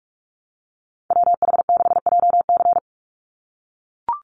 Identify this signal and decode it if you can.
Morse code, heard as a keyed tone, about 44 wpm, 700 Hz. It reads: WH6JC
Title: USA Call Signs 45wpm